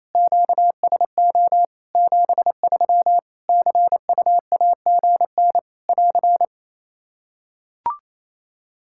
Morse code, heard as a keyed tone, about 28 wpm, 700 Hz. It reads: QSO 73 CUAGN <AR>
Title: K1USN SST QSO Practice 28wpm